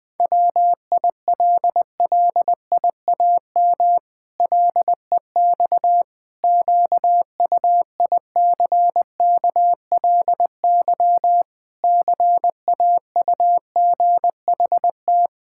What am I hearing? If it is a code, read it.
Morse code, heard as a keyed tone, about 20 wpm, 700 Hz. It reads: WILLIAM LEX QUICKLY CAUGHT
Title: Pangram Challenge v2 - 20wpm